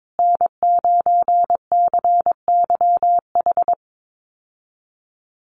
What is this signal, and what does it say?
Morse code, heard as a keyed tone, about 22 wpm, 700 Hz. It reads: N9CYH
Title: USA Call Signs Repeated 3x 22wpm